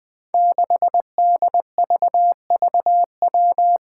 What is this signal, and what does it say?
Morse code, heard as a keyed tone, about 20 wpm, 700 Hz. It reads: 6DVVW
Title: Five Challenging Letter-Number Combinations 20wpm